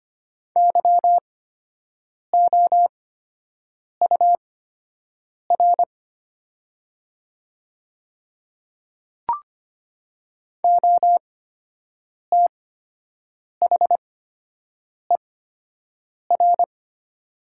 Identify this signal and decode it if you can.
Morse code, heard as a keyed tone, about 25 wpm, spaced as for 7 wpm, 700 Hz. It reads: YOUR OTHER
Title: 8x Character Spacing - Top 100 Words - 25wpm